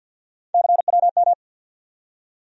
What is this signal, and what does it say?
Morse code, heard as a keyed tone, about 49 wpm, 700 Hz. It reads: CWM
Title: USA Call Sign Postfixes - 3 Letters 50wpm